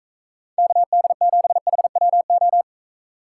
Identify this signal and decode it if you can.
Morse code, heard as a keyed tone, about 45 wpm, 700 Hz. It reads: KD7HWO